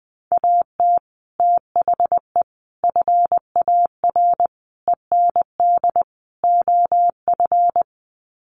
Morse code, heard as a keyed tone, about 20 wpm, 700 Hz. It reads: AT THE FAR END OF
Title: Sets of 5 Words 20wpm